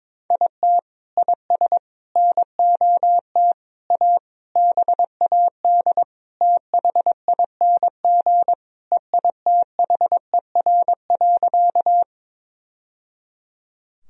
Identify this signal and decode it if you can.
Morse code, heard as a keyed tone, about 22 wpm, 700 Hz. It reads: IT IS NOT A BAD THING EITHER.
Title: Sentences from Top 1000 Words - Review Part 4 - 22wpm